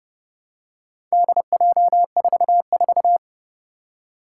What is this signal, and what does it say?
Morse code, heard as a keyed tone, about 30 wpm, 700 Hz. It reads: DJ44
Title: Four Challenging Letter-Number Combinations 30wpm